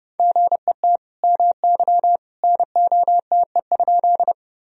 Morse code, heard as a keyed tone, about 31 wpm, 700 Hz. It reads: GET MY NOTE?
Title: Sentences from Top 1000 Words - Review Part 2 - 30wpm